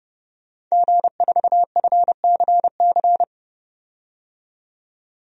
Morse code, heard as a keyed tone, about 30 wpm, 700 Hz. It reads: G4FCC
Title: DXCC-100 Call Signs - 5 Characters - 30wpm